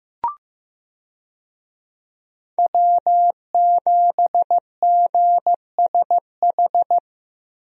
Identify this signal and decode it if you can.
Morse code, heard as a keyed tone, about 15 wpm, 700 Hz. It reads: W7GSH
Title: USA Call Signs - 5 Characters - Repeated 2x - 15wpm